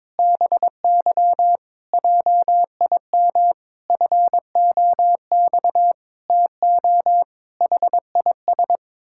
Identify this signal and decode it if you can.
Morse code, heard as a keyed tone, about 22 wpm, 700 Hz. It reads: BY JIM FOX TO HIS
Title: Pangram Challenge v2 - 22wpm